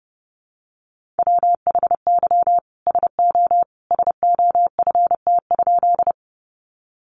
Morse code, heard as a keyed tone, about 30 wpm, 700 Hz. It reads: WHY SO SOFT?
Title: Sentences from Top 1500 Words 30wpm